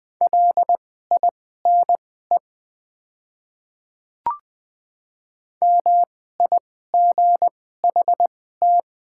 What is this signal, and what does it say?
Morse code, heard as a keyed tone, about 20 wpm, spaced as for 14 wpm, 700 Hz. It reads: LINE MIGHT
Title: 2x Character Spacing - Top 500 Words - 20wpm